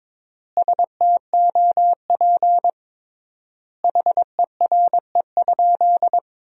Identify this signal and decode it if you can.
Morse code, heard as a keyed tone, about 21 wpm, 700 Hz. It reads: STOP HERE?
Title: 3x Word-Spacing - Sentences from Top 300 Words - 22wpm